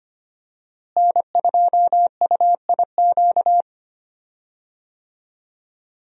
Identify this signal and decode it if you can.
Morse code, heard as a keyed tone, about 25 wpm, 700 Hz. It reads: N2UIQ
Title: USA Call Signs - 5 Characters - Repeated 2x - 25wpm